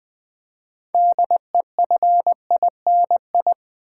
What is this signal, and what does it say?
Morse code, heard as a keyed tone, about 20 wpm, 700 Hz. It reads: DEFINI
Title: Build A Word - 2 Letters At A Time - 8-Letter Words - 20wpm